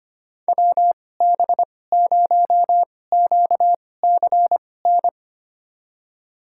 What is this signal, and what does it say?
Morse code, heard as a keyed tone, about 25 wpm, spaced as for 18 wpm, 700 Hz. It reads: WB0QCN
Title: USA Call Signs - 6 Characters - 2x Character Spacing - 25wpm